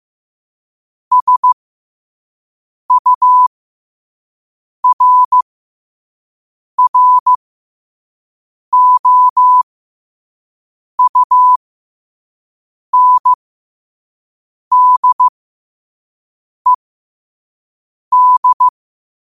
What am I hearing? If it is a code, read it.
Morse code, heard as a keyed tone, about 15 wpm, spaced as for 5 wpm, 1000 Hz. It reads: SURROUNDED
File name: qotdfiles / 5wpm